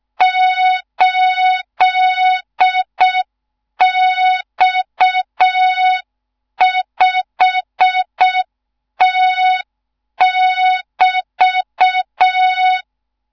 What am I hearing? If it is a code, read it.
Morse code, heard as a keyed tone, about 6 wpm, 3700 Hz. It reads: 8X5T<BT>
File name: Morze-HA2MN / hang / 14 280 HA2MN